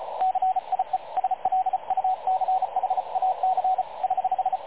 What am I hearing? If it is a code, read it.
Morse code, heard as a keyed tone, about 35 wpm, 750 Hz, with radio noise. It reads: CIE SLUCHAC <HH>
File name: dlazaawansowanych / nagraniazpasma / websdr recording 3567.0kHz